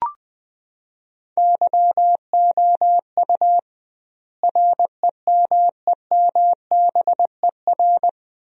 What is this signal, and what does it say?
Morse code, heard as a keyed tone, about 20 wpm, 700 Hz. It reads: YOU REMEMBER
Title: 2x Word-Spacing - Sentences from Top 400 Words - 20wpm